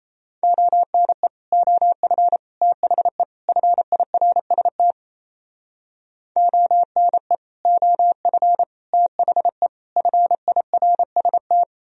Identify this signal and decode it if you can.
Morse code, heard as a keyed tone, about 38 wpm, 700 Hz. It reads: ONE OF THE FIRST ONE OF THE FIRST